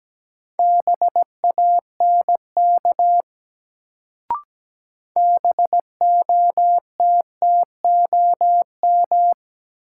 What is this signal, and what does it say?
Morse code, heard as a keyed tone, about 17 wpm, 700 Hz. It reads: BANK BOTTOM
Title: Sentences from Top 900 Words - New Word Drill - 17wpm